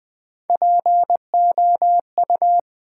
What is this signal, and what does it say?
Morse code, heard as a keyed tone, about 20 wpm, 700 Hz. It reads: POU
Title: Build A Word - 1 Letter At A Time - 5-Letter Words - 20wpm